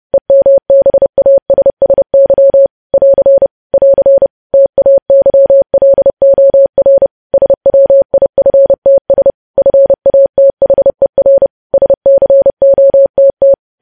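Morse code, heard as a keyed tone, about 30 wpm, 550 Hz. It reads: EMBASSY <AR> <AR> TAYLOR SWIFTS FATHER SCOTT
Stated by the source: News headlines sent in Morse at 30 WPM.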